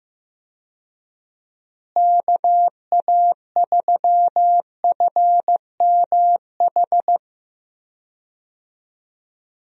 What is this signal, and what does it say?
Morse code, heard as a keyed tone, about 15 wpm, 700 Hz. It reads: KA3FMH
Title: USA Call Signs - 6 Characters - Repeated 2x - 15wpm